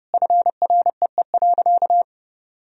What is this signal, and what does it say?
Morse code, heard as a keyed tone, about 32 wpm, 700 Hz. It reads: FREE.
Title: Sentences from Top 1000 Words - Review Part 1 - 30wpm